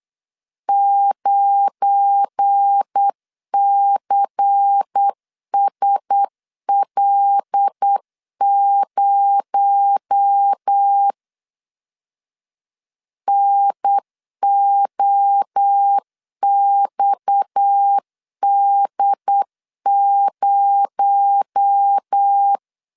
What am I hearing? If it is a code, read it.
Morse code, heard as a keyed tone, about 8 wpm, 800 Hz. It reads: 9CSL0 NOXD0
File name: LEZIONI / 7 GF9Z / 7 GF9 8wpm 3545